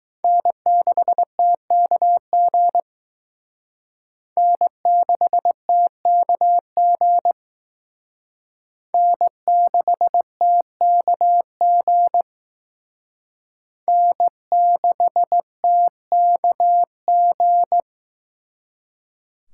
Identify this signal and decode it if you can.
Morse code, heard as a keyed tone, about 19 wpm, 700 Hz. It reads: N6TKG N6TKG N6TKG N6TKG